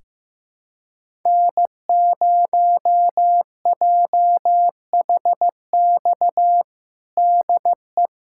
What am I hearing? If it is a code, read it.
Morse code, heard as a keyed tone, about 15 wpm, 700 Hz. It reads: N0JHX DE
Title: USA Callsign DE Callsign - 5 Characters - 15wpm